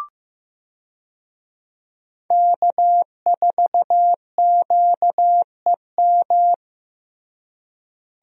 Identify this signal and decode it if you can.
Morse code, heard as a keyed tone, about 16 wpm, 700 Hz. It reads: K4QEM
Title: USA Call Signs - 5 Characters - Repeated 2x - 15wpm